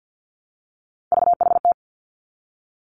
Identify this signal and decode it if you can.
Morse code, heard as a keyed tone, about 50 wpm, 700 Hz. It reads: VHT
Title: Three Challenging Letter-Number Combinations 50wpm